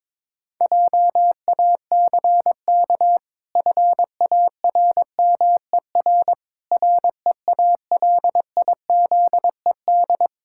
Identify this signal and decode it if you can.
Morse code, heard as a keyed tone, about 22 wpm, 700 Hz. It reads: JACK FARMER REALIZED
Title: Pangram Challenge v2 - 22wpm